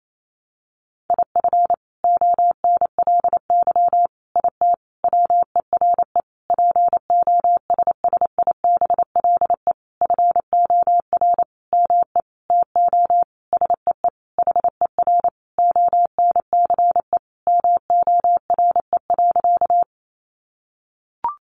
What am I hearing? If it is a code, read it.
Morse code, heard as a keyed tone, about 28 wpm, 700 Hz. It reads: IF ONLY IT WERE POSSIBLE FOR ME TO SEE HER ONCE MORE.
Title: Sentences from Top 500 Words 28wpm